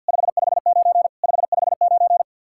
Morse code, heard as a keyed tone, about 66 wpm, spaced as for 58 wpm, 700 Hz. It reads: 559 559
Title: Speed-Racing Common QSO Phrases Encore - 50wpm